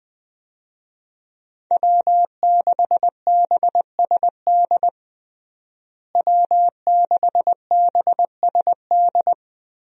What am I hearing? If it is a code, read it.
Morse code, heard as a keyed tone, about 20 wpm, 700 Hz. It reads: W6BSD W6BSD
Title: USA Call Signs - 5 Characters - Repeated 3x - 20wpm